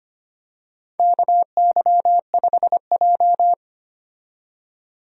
Morse code, heard as a keyed tone, about 25 wpm, 700 Hz. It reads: KY5J
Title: USA Call Signs - 4 Characters - 25wpm